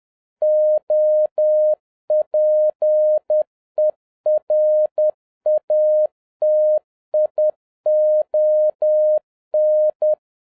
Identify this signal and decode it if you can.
Morse code, heard as a keyed tone, about 10 wpm, 610 Hz. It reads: OPERATION
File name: 2011-07-mp3 / n41 EXCITING QRP CW OPERATION 10wpm